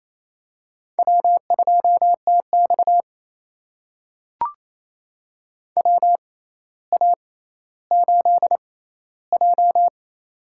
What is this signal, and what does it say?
Morse code, heard as a keyed tone, about 28 wpm, spaced as for 14 wpm, 700 Hz. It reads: W2TX WA8J
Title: USA Call Signs - 4 Characters - 6x Character Spacing - 28wpm